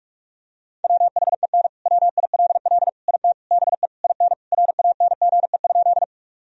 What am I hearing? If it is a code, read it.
Morse code, heard as a keyed tone, about 43 wpm, 700 Hz. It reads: WHEN WILL IT BE IN RANGE?
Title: Sentences from Top 1000 Words - Review Part 4 - 45wpm